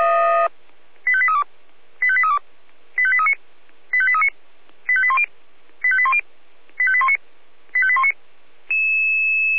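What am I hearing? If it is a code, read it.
Five-tone selective call (ZVEI), heard as quick runs of five tones, tones of about 70 ms. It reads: 85832 85832 85839 85839 85810 85810 85819 85819
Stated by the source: ZVEI five-tone selective call received on an SDR.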